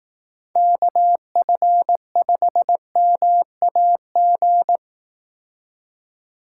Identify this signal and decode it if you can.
Morse code, heard as a keyed tone, about 18 wpm, 700 Hz. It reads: KF5MAG
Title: Speed Racing - USA Call Signs 15wpm